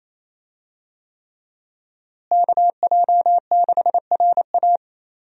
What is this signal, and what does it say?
Morse code, heard as a keyed tone, about 28 wpm, 700 Hz. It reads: KJ6RA
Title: USA Call Signs - 5 Characters - Repeated 3x - 28wpm